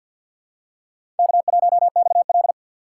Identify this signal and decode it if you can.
Morse code, heard as a keyed tone, about 47 wpm, 700 Hz. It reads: K1XL
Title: USA Call Signs - 4 Characters - 3x Character Spacing - 50wpm